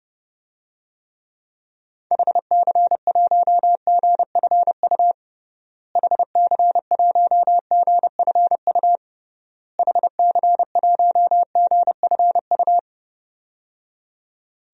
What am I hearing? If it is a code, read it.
Morse code, heard as a keyed tone, about 31 wpm, 700 Hz. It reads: HC1GFU HC1GFU HC1GFU
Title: DXCC-100 Call Signs - 6 Characters - Repeated 3x - 30wpm